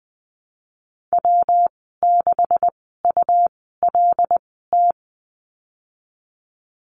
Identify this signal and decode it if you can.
Morse code, heard as a keyed tone, about 20 wpm, spaced as for 15 wpm, 700 Hz. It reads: W6ULT
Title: 147 Lesson 24g - Letter y - Call Signs 20wpm